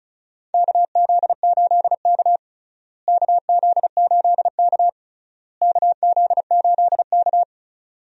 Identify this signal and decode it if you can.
Morse code, heard as a keyed tone, about 36 wpm, 700 Hz. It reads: KZ8K KZ8K KZ8K
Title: USA Call Signs - 4 Characters - Repeated 3x - 35wpm